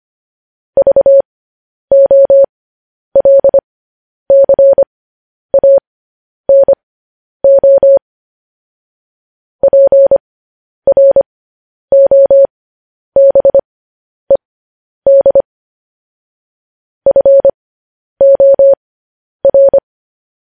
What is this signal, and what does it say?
Morse code, heard as a keyed tone, about 25 wpm, spaced as for 10 wpm, 550 Hz. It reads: VOLCANO PROBED FOR
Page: News Headlines in Morse Code at 10 WPM